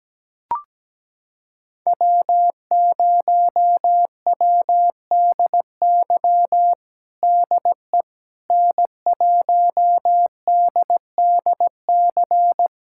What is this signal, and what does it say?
Morse code, heard as a keyed tone, about 17 wpm, 700 Hz. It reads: W0WDY DE N1DDC
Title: USA Callsign DE Callsign - 5 Characters - 17wpm